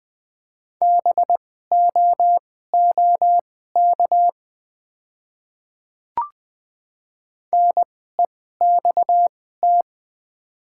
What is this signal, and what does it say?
Morse code, heard as a keyed tone, about 20 wpm, spaced as for 14 wpm, 700 Hz. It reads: BOOK NEXT
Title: 2x Character Spacing - Top 300 Words - 20wpm